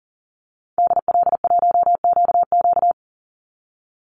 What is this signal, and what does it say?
Morse code, heard as a keyed tone, about 40 wpm, 700 Hz. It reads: DL1QQ
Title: Call Signs - 40wpm